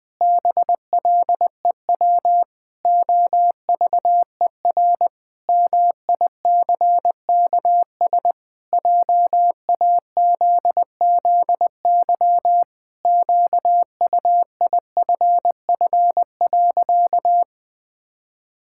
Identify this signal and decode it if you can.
Morse code, heard as a keyed tone, about 20 wpm, 700 Hz. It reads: BLEW OVER MICKS JAZZY QUIFF.
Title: Pangram Challenge v2 - 20wpm